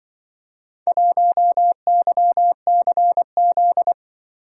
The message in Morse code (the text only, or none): 1YCZ